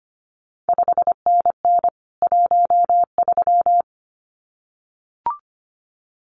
Morse code, heard as a keyed tone, about 25 wpm, 700 Hz. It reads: 5NN 13
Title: CQ WW DX Contest Exchange Encore - 25wpm